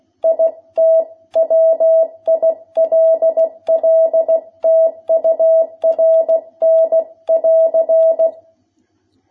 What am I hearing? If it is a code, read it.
Morse code, heard as a keyed tone, about 16 wpm, 660 Hz. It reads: IT WILL TURN <AR>
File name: jh8cbh / 2021cw